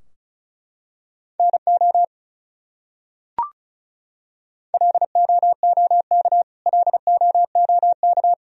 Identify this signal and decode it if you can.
Morse code, heard as a keyed tone, about 36 wpm, 700 Hz. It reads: NO LOOK LOOK
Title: Top 100 Words Repeated 35wpm